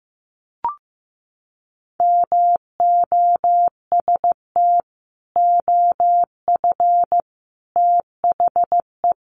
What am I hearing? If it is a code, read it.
Morse code, heard as a keyed tone, about 15 wpm, 700 Hz. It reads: MOST OF THE
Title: Sets of 4 Words 15wpm